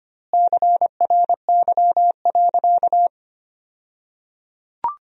Morse code, heard as a keyed tone, about 25 wpm, 700 Hz. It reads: CRY.
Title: 2x Word-Spacing - Sentences from Top 500 Words - 25wpm